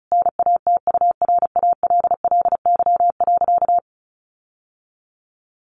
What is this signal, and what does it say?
Morse code, heard as a keyed tone, about 34 wpm, 700 Hz. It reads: NATURALLY.
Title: Sentences from Top 1300 Words 35wpm